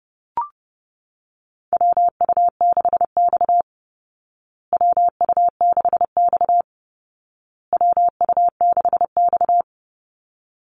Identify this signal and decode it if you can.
Morse code, heard as a keyed tone, about 30 wpm, 700 Hz. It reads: WU6X WU6X WU6X
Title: Call Signs Repeated 2x - 30wpm